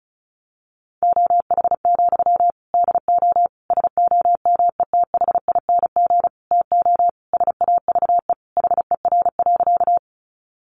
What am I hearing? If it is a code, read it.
Morse code, heard as a keyed tone, about 34 wpm, 700 Hz. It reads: OH, DO SOMETHING TO SAVE HER.
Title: Sentences from Top 1100 Words 35wpm